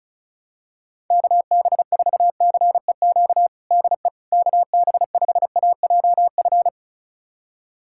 Morse code, heard as a keyed tone, about 34 wpm, 700 Hz. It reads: KB4CEQ DE KB5AJF
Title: USA Callsign DE Callsign 35wpm